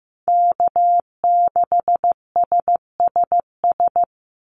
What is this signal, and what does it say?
Morse code, heard as a keyed tone, about 15 wpm, 700 Hz. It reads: K6SSS
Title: Call Signs Repeated 3x - 15wpm